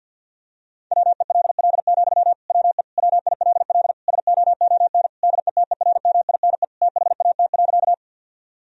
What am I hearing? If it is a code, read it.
Morse code, heard as a keyed tone, about 50 wpm, 700 Hz. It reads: WELL, WE WILL SOON DETERMINE THAT.
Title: Sentences from Top 1000 Words - Review Part 4 - 50wpm